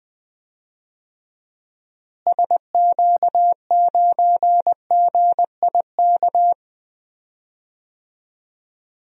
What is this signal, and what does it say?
Morse code, heard as a keyed tone, about 20 wpm, 700 Hz. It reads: SQ9GIK
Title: DXCC-100 Call Signs - 6 Characters - Repeated 3x - 20wpm